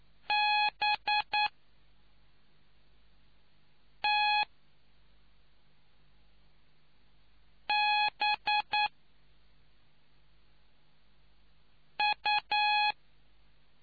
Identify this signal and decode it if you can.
Morse code, heard as a keyed tone, about 9 wpm, 840 Hz. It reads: B T B U